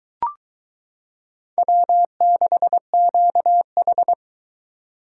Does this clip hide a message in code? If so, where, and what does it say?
Yes: Morse code from 1.6 to 4.1 s, heard as a keyed tone, about 22 wpm, 700 Hz. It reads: W6QH